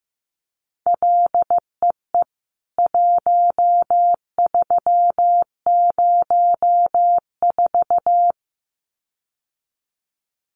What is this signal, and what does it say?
Morse code, heard as a keyed tone, about 15 wpm, 700 Hz. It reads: LEE 1304
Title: CWT Exchange 15wpm